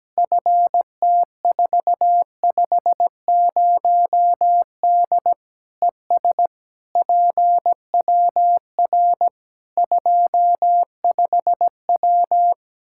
Morse code, heard as a keyed tone, about 17 wpm, 700 Hz. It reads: FT450D ES PWR 25W
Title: K1USN SST QSO Practice 17wpm